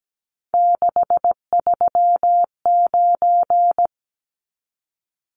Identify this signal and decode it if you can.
Morse code, heard as a keyed tone, about 17 wpm, 700 Hz. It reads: 639
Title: Numbers - 3-Digit 17wpm